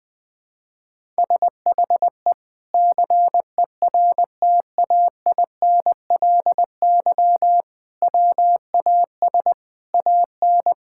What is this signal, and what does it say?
Morse code, heard as a keyed tone, about 20 wpm, 700 Hz. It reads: SHE CERTAINLY WAS AN
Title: Sentences from Top 1000 Words - Review Part 6 - 20wpm